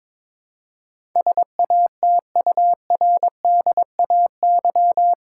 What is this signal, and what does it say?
Morse code, heard as a keyed tone, about 21 wpm, 700 Hz. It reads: SATURDAY
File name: practice / 22wpm / Eight Letter Words - 22wpm